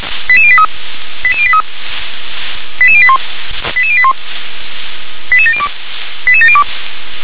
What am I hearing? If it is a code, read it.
Five-tone selective call (ZVEI), heard as quick runs of five tones, tones of about 70 ms. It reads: 88083 88083 88081 88081 88702 88702